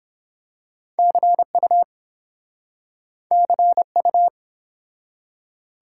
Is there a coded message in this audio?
Yes: Morse code from 1.0 to 4.3 s, heard as a keyed tone, about 31 wpm, 700 Hz. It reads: CU CU